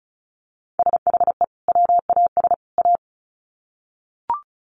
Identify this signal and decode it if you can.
Morse code, heard as a keyed tone, about 34 wpm, 700 Hz. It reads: SHE WAS A
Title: Sets of 3 Words 35wpm